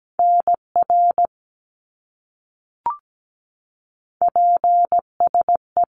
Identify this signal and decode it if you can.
Morse code, heard as a keyed tone, about 17 wpm, 700 Hz. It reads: NR PSE
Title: Abbreviations Repeated 3x - 17wpm